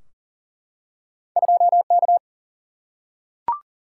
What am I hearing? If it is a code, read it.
Morse code, heard as a keyed tone, about 38 wpm, 700 Hz. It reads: 2K
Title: Single-Letter Single-Number 40wpm